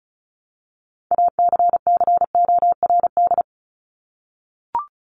Morse code, heard as a keyed tone, about 34 wpm, 700 Hz. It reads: ACCORD
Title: ING Suffix - Top 100 Words with Base Words 35wpm